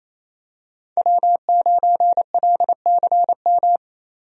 Morse code, heard as a keyed tone, about 28 wpm, 700 Hz. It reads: W9LCM
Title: USA Call Signs - 5 Characters - 28wpm